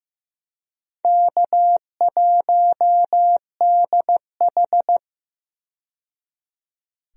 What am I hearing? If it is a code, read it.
Morse code, heard as a keyed tone, about 15 wpm, 700 Hz. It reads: K1DH
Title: USA Call Signs - 4 Characters - 15wpm